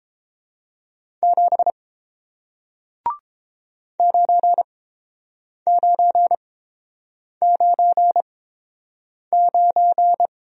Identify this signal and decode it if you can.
Morse code, heard as a keyed tone, about 30 wpm, spaced as for 7 wpm, 700 Hz. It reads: 7 9999